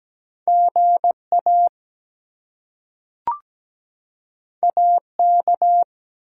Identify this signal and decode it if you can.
Morse code, heard as a keyed tone, about 17 wpm, 700 Hz. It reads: GA AK
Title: US States and Canadian Provinces 17wpm